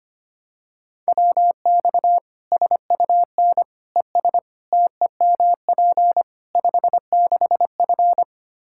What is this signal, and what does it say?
Morse code, heard as a keyed tone, about 25 wpm, 700 Hz. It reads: WX SUN ES TEMP 56F
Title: Practice QSO - Exchange 2 - Info - 25wpm